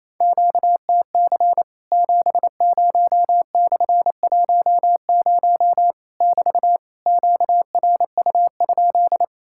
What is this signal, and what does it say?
Morse code, heard as a keyed tone, about 28 wpm, 700 Hz. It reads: QTC 70/10 <BT> QRU?
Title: Worked All Europe DX Contest - QTC Exchange - 28wpm